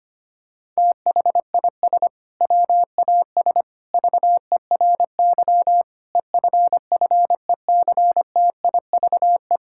Morse code, heard as a keyed tone, about 25 wpm, 700 Hz. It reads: THIS WAS VERY EFFECTIVE
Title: Sentences from Top 1000 Words - Review Part 6 - 25wpm